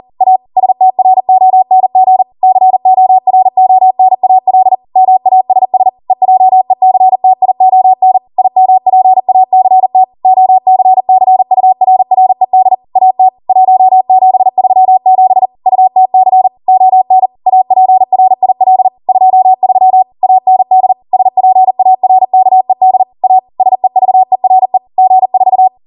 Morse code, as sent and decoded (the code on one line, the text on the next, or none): .- / ... - .-. --- -. --. / -.-. --- .-. --- -. .- .-.. / -- .- ... ... / . .--- . -.-. - .. --- -. / .. -- .--. .- -.-. - / --- -.-. -.-. ..- .-. .-. -.. / .- - / .---- --... ...-- --... / ..- - -.-. / --- -. / .- .--. .-. .. .-.. / ..--- ...-- / .- -. -.. / ... .--. .- .-. -.- . -.. / .- / ... ...- . .-. / --. ....-
A STRONG CORONAL MASS EJECTION IMPACT OCCURRED AT 1737 UTC ON APRIL 23 AND SPARKED A SEVERE G4